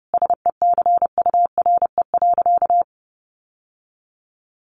Morse code, heard as a keyed tone, about 30 wpm, 700 Hz. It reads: SECURE.
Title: Sentences from Top 1200 Words 30wpm